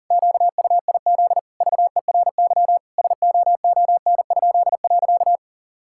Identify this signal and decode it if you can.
Morse code, heard as a keyed tone, about 40 wpm, 700 Hz. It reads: QUIZ VERY SOON?.
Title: Pangram Challenge v2 - 40wpm